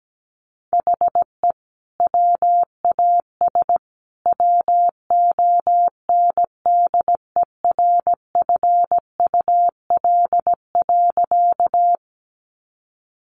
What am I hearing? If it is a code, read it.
Morse code, heard as a keyed tone, about 17 wpm, 700 Hz. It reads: HE WAS WONDERFUL.
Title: Sentences from Top 1100 Words 17wpm